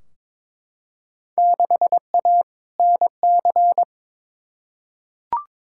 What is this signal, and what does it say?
Morse code, heard as a keyed tone, about 22 wpm, 700 Hz. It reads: 6A NC
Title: Field Day Exchanges Encore - 22wpm